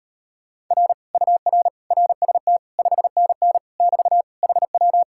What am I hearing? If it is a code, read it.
Morse code, heard as a keyed tone, about 37 wpm, 700 Hz. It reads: R UR RST 5NN <BT> HW